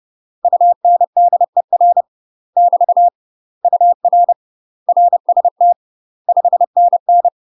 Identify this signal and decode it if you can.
Morse code, heard as a keyed tone, about 30 wpm, 700 Hz. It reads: UNDER <BT> UR RST 5NN
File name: UK MASTER QSOs TRAINING / 30WPM / UK MASTER QSOs 0003